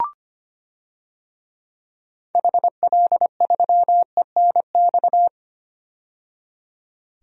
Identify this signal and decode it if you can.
Morse code, heard as a keyed tone, about 25 wpm, 700 Hz. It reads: HL3ENX
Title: DXCC-100 Call Signs - 6 Characters - 25wpm